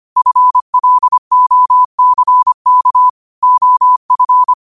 Morse code, heard as a keyed tone, about 25 wpm, 1000 Hz. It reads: FLOCK OF
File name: qotdfiles / 25wpm